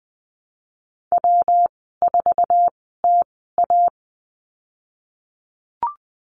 Morse code, heard as a keyed tone, about 20 wpm, spaced as for 14 wpm, 700 Hz. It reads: W4TA
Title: 147 Lesson 24g - Letter y - Call Signs 20wpm